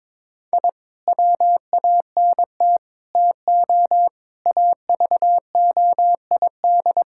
Morse code, heard as a keyed tone, about 22 wpm, 700 Hz. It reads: I WANT TO AVOID
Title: Sentences from Top 1000 Words - Review Part 6 - 22wpm